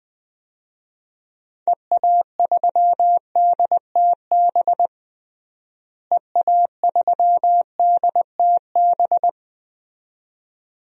Morse code, heard as a keyed tone, about 21 wpm, 700 Hz. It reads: EA3DTB EA3DTB
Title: DXCC-100 Call Signs - 6 Characters - Repeated 2x - 20wpm